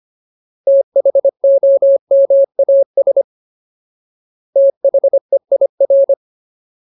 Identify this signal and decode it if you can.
Morse code, heard as a keyed tone, about 25 wpm, 550 Hz. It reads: THOMAS THEIR
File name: Ultimate Word List / 25WPM Ultimate